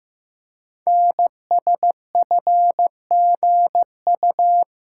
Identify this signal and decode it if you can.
Morse code, heard as a keyed tone, about 15 wpm, 700 Hz. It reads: NSFGU
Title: Code Groups - 5 Letters - 15wpm